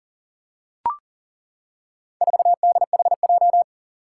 Morse code, heard as a keyed tone, about 40 wpm, 700 Hz. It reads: 4DHJ
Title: Four Challenging Letter-Number Combinations 40wpm